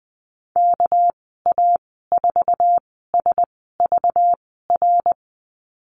Morse code, heard as a keyed tone, about 20 wpm, spaced as for 14 wpm, 700 Hz. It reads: KA4SVR
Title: 207 Lesson 33g - Letter j - Call Signs 20wpm